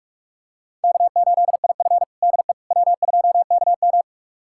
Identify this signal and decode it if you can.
Morse code, heard as a keyed tone, about 46 wpm, 700 Hz. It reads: K8IF DE W2KM
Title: USA Callsign DE Callsign - 4 Characters - 45wpm